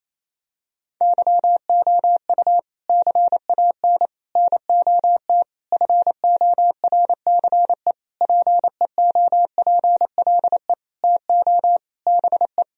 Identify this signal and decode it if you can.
Morse code, heard as a keyed tone, about 27 wpm, 700 Hz. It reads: YOU CAN NOT FORCE PEOPLE TO BE
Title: Sentences from Top 1000 Words - Review Part 3 - 28wpm